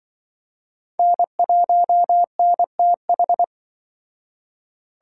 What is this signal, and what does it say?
Morse code, heard as a keyed tone, about 25 wpm, 700 Hz. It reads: N1NTH